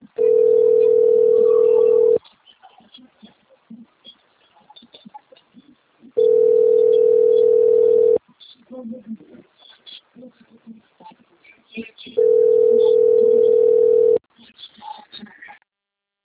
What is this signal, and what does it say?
Ringback tone (North America), 440 + 480 Hz, with speech.
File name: rg-1600-unknown- - - .36060